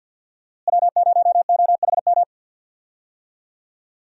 Morse code, heard as a keyed tone, about 50 wpm, 700 Hz. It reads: W0OHM
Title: USA Call Signs - 5 Characters - Repeated 2x - 50wpm